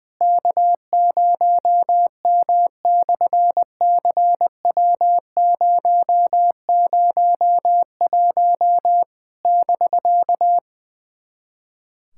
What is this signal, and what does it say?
Morse code, heard as a keyed tone, about 20 wpm, 700 Hz. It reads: K0M/CW001 <BK>
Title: SOTA QSO Practice 20wpm